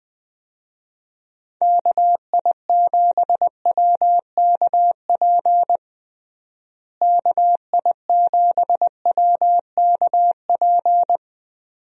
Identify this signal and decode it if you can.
Morse code, heard as a keyed tone, about 20 wpm, 700 Hz. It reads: KI7WKP KI7WKP
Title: USA Call Signs - 6 Characters - Repeated 2x - 20wpm